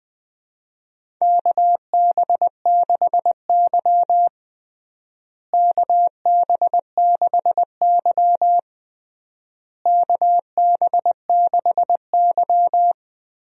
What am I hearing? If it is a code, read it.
Morse code, heard as a keyed tone, about 20 wpm, 700 Hz. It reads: KB6Y KB6Y KB6Y
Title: USA Call Signs - 4 Characters - Repeated 3x - 20wpm